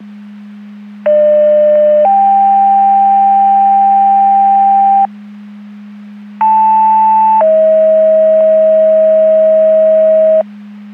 Two-tone page, heard as a one-second tone, then a three-second tone, about 600 then 790 Hz, about 880 then 640 Hz. It is heard from 1.1 to 10.4 s.
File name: Portage 154.385 Paging